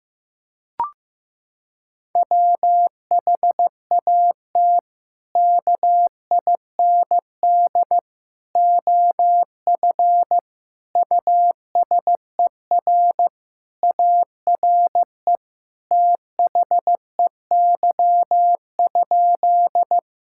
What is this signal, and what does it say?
Morse code, heard as a keyed tone, about 15 wpm, 700 Hz. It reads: WHAT KIND OF USER ARE THEY?
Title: Sentences from Top 1000 Words - Review Part 5 - 15wpm